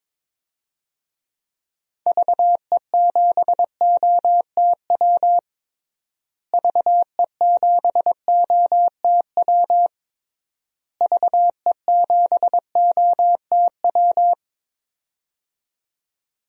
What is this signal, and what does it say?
Morse code, heard as a keyed tone, about 22 wpm, 700 Hz. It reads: VE7OTW VE7OTW VE7OTW
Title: DXCC-100 Call Signs - 6 Characters - Repeated 3x - 22wpm